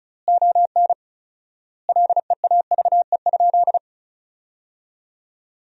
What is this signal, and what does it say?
Morse code, heard as a keyed tone, about 34 wpm, 700 Hz. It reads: ON LEAVE?
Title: 4x Word-Spacing - Sentences from Top 400 Words - 35wpm